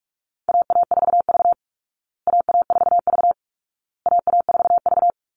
Morse code, heard as a keyed tone, about 44 wpm, 700 Hz. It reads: AA4V AA4V AA4V
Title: Call Signs Repeated 2x - 45wpm